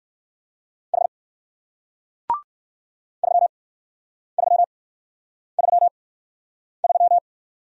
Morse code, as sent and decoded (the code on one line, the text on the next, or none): .... / ...-- ...-- ...-- ...--
H 3333